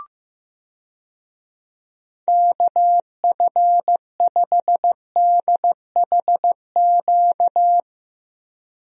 Morse code, heard as a keyed tone, about 16 wpm, 700 Hz. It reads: KF5DHQ
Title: USA Call Signs - 6 Characters - Repeated 3x - 15wpm